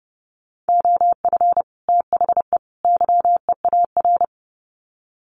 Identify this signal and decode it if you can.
Morse code, heard as a keyed tone, about 30 wpm, 700 Hz. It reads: OF THE YEAR
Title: Sets of 3 Words 30wpm